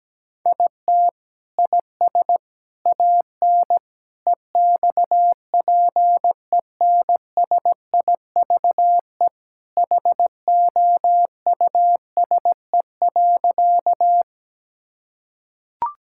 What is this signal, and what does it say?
Morse code, heard as a keyed tone, about 17 wpm, 700 Hz. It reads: IT IS AN EXPENSIVE HOUSE.
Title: Sentences from Top 1000 Words - Review Part 6 - 17wpm